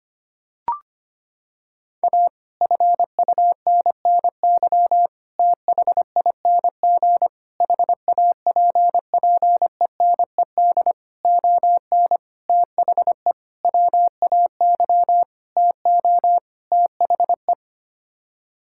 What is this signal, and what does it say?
Morse code, heard as a keyed tone, about 25 wpm, 700 Hz. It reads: A FUNNY THING HAPPENED ON THE WAY TO THE
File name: Sets of 9 Words - 25wpm